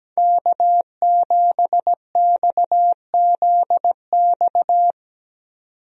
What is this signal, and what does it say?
Morse code, heard as a keyed tone, about 17 wpm, 700 Hz. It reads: K7XZX
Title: USA Call Signs - 5 Characters - 17wpm